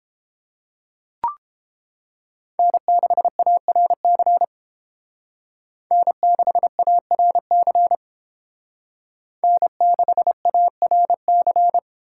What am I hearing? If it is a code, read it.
Morse code, heard as a keyed tone, about 30 wpm, 700 Hz. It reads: N6ARC N6ARC N6ARC